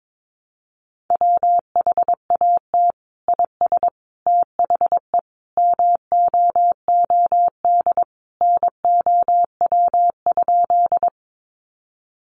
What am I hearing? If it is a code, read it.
Morse code, heard as a keyed tone, about 22 wpm, 700 Hz. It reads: WHAT IS THE MOOD NOW?
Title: Sentences from Top 1100 Words 22wpm